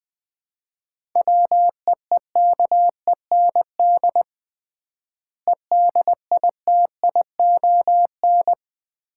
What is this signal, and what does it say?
Morse code, heard as a keyed tone, about 21 wpm, 700 Hz. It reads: WEEKEND EDITION
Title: 3x Word-Spacing - Sets of 5 Words - 20wpm